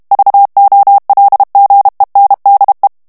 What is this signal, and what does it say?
Morse code, heard as a keyed tone, about 32 wpm, 800 Hz. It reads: VOLGENDE
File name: HOLLANDAIS texte clair 2 vitesse 30 mots minute